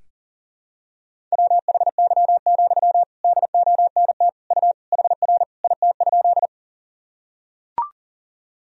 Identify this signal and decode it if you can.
Morse code, heard as a keyed tone, about 37 wpm, 700 Hz. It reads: WHY, DONT U HR IT?
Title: Sentences from Top 300 Words and Abbreviations 40wpm